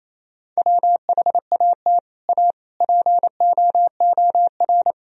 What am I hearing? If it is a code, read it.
Morse code, heard as a keyed tone, about 28 wpm, 700 Hz. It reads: WHAT A POOR
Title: Sentences from Top 1000 Words - Review Part 1 - 28wpm